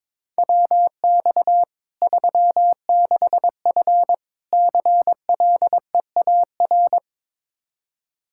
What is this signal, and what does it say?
Morse code, heard as a keyed tone, about 22 wpm, 700 Hz. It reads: WX 36F CLEAR
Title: Practice QSO - Exchange 2 - Info - 22wpm